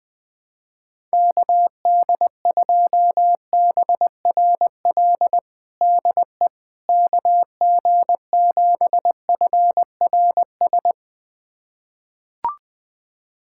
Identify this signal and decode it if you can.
Morse code, heard as a keyed tone, about 21 wpm, 700 Hz. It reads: KD2BRL DE KG7FRS
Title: USA Callsign DE Callsign 20wpm